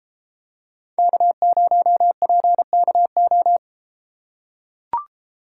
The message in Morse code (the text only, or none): K0PKO